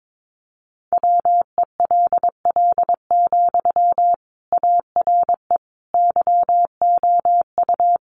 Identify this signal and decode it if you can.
Morse code, heard as a keyed tone, about 22 wpm, 700 Hz. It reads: WELL, ARE YOU
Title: Sentences from Top 800 Words 22wpm